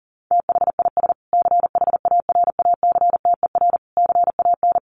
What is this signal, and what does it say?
Morse code, heard as a keyed tone, about 40 wpm, 700 Hz. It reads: THIS CHARACTER CAN
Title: Sets of 5 Words Encore2 40wpm